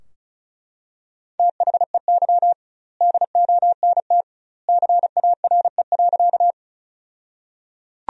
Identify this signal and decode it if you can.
Morse code, heard as a keyed tone, about 36 wpm, 700 Hz. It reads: THEY DONT CARE.
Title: 2x Word-Spacing - Sentences from Top 300 Words - 35wpm